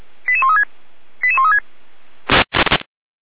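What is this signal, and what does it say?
Five-tone selective call (ZVEI), heard as quick runs of five tones, tones of about 70 ms. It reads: 88136 88136
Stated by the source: Five-tone ZVEI selective call received by SDR, Feuerwehr Orlamünde.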